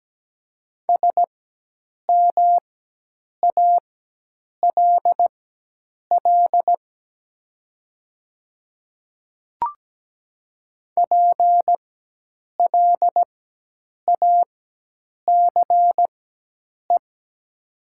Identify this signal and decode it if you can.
Morse code, heard as a keyed tone, about 17 wpm, spaced as for 8 wpm, 700 Hz. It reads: SMALL PLACE
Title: 4x Character Spacing - Top 200 Words - 17wpm